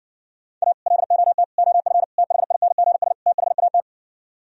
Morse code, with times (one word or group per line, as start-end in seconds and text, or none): 0.6-0.7 A
0.9-1.4 LOT
1.6-2.0 OF
2.2-3.1 THINGS
3.3-3.8 THAT